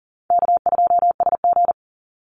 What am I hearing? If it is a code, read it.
Morse code, heard as a keyed tone, about 40 wpm, 700 Hz. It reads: K2SG
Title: Easy Call Signs Repeated 2x - 40wpm